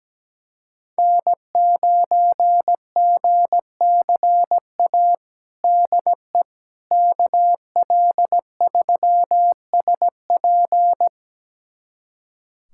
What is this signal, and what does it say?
Morse code, heard as a keyed tone, about 17 wpm, 700 Hz. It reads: N9GCA DE KL3SP
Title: USA Callsign DE Callsign - 5 Characters - 17wpm